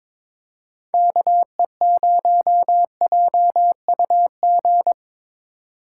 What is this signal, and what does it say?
Morse code, heard as a keyed tone, about 22 wpm, 700 Hz. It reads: KE0JUG
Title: USA Call Signs - 4x Character Spacing - 22wpm